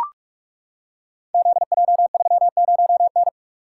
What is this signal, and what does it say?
Morse code, heard as a keyed tone, about 44 wpm, 700 Hz. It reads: ZJ30N
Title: Code Groups - 5 Letter:Numbers - 45wpm